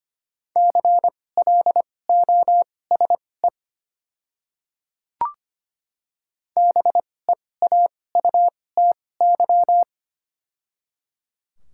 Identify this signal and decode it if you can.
Morse code, heard as a keyed tone, about 25 wpm, spaced as for 18 wpm, 690 Hz. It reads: CLOSE BEAUTY
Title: 2x Character Spacing - Top 500 Words - 25wpm